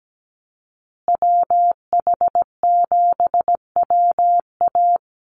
Morse code, heard as a keyed tone, about 17 wpm, 700 Hz. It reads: WH7WA
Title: USA Call Signs Repeated 2x 17wpm